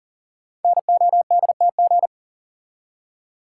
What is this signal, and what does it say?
Morse code, heard as a keyed tone, about 38 wpm, 700 Hz. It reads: NODTG
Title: Code Groups - 5 Letters - 40wpm